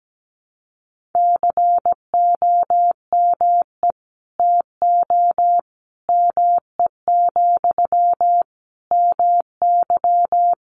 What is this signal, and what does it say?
Morse code, heard as a keyed tone, about 17 wpm, 700 Hz. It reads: COME TO ME, MY
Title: Sentences from Top 1500 Words 17wpm